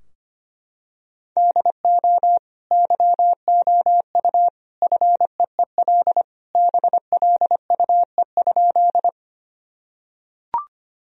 Morse code, heard as a keyed tone, about 25 wpm, 700 Hz. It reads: DO YOU FEEL BLUE?
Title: Sentences from Top 1000 Words - Review Part 3 - 25wpm